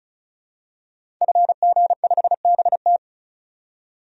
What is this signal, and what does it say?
Morse code, heard as a keyed tone, about 34 wpm, 700 Hz. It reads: FG5BT
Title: DXCC-100 Call Signs - 5 Characters - 35wpm